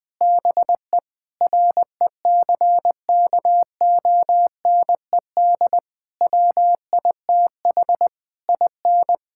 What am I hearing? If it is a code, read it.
Morse code, heard as a keyed tone, about 21 wpm, 700 Hz. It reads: BE RECKONED WITH IN
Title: Sentences with 6-Word Phrases - 20wpm